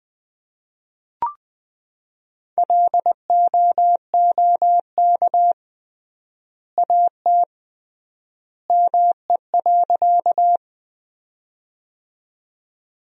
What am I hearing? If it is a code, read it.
Morse code, heard as a keyed tone, about 20 wpm, 700 Hz. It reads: LOOK AT ME.
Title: 3x Word-Spacing - Sentences from Top 200 Words - 20wpm